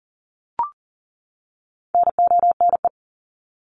Morse code, heard as a keyed tone, about 40 wpm, 700 Hz. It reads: NONE
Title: Top 1501-1750 Words 40wpm